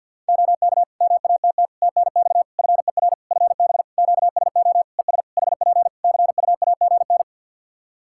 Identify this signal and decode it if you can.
Morse code, heard as a keyed tone, about 46 wpm, 700 Hz. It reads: OK MATT TNX FER FB QSO ES HP CUAGN
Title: Practice QSO - Exchange 4 - Ending - 50wpm